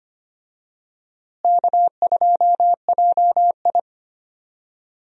Morse code, heard as a keyed tone, about 25 wpm, 700 Hz. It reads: K2JI
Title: USA Call Signs - 4 Characters - Repeated 3x - 25wpm